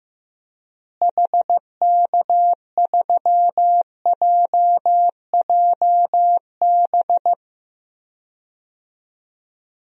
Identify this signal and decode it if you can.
Morse code, heard as a keyed tone, about 16 wpm, 700 Hz. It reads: HK3JJB
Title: DXCC-100 Call Signs - 6 Characters - Repeated 2x - 15wpm